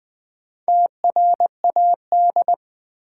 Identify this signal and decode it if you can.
Morse code, heard as a keyed tone, about 20 wpm, 700 Hz. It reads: TRAD
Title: Build A Word - 1 Letter At A Time - 5-Letter Words - 20wpm